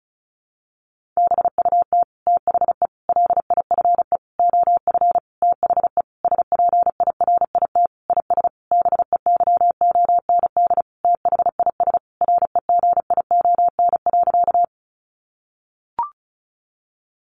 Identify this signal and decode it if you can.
Morse code, heard as a keyed tone, about 34 wpm, 700 Hz. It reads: BUT THE LIFE OF THE SPIRIT IS BEYOND THIS REGION.
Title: Sentences from Top 1500 Words 35wpm